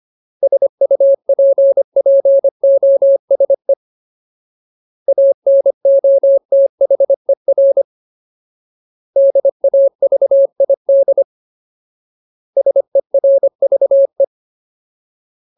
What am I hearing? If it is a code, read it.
Morse code, heard as a keyed tone, about 25 wpm, 550 Hz. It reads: SUPPOSE ANOTHER DAVID SERVE
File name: Ultimate Word List / 25WPM Ultimate